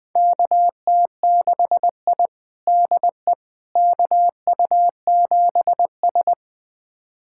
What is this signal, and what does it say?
Morse code, heard as a keyed tone, about 20 wpm, 700 Hz. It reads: KT6I DE KU7S
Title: USA Callsign DE Callsign - 4 Characters - 20wpm